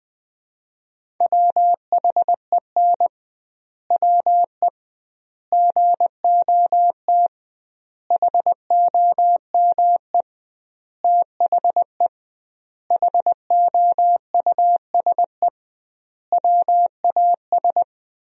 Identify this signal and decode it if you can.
Morse code, heard as a keyed tone, about 19 wpm, 700 Hz. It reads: WHEN WE GOT HOME THE HOUSE WAS
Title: 2x Word-Spacing - Sentences from Top 500 Words - 20wpm